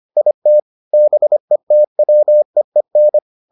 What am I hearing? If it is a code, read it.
Morse code, heard as a keyed tone, about 25 wpm, 600 Hz. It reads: IT BETWEEN